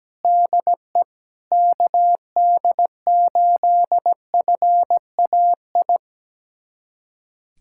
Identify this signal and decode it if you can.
Morse code, heard as a keyed tone, about 17 wpm, 700 Hz. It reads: DE KD8FAI
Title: USA Callsign DE Callsign 17wpm